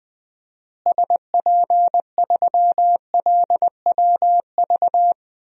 Morse code, heard as a keyed tone, about 21 wpm, 700 Hz. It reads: SP3LWV
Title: DXCC-100 Call Signs - 6 Characters - Repeated 2x - 20wpm